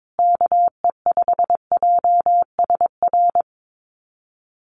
Morse code, heard as a keyed tone, about 22 wpm, 700 Hz. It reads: KE5JSR
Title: USA Call Signs Repeated 3x 22wpm